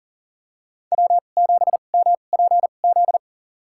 Morse code, heard as a keyed tone, about 40 wpm, spaced as for 29 wpm, 700 Hz. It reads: W7MPZ
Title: USA Call Signs - 5 Characters - 2x Character Spacing - 40wpm